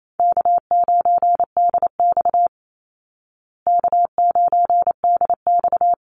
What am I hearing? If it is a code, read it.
Morse code, heard as a keyed tone, about 28 wpm, 700 Hz. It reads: K9DX K9DX
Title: Easy Call Signs Repeated 2x - 28wpm